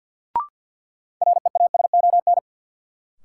Morse code, heard as a keyed tone, about 54 wpm, 700 Hz. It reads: REASON
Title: Sentences from Top 700 Words - New Word Drill - 50wpm